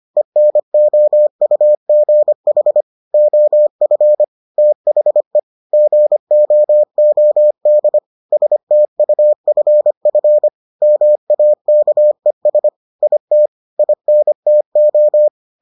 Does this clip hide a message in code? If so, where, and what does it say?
Yes: Morse code from 0.2 to 15.3 s, heard as a keyed tone, about 25 wpm, 600 Hz. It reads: ENOUGH OF THE GOOD STUFF MAKES IT INTO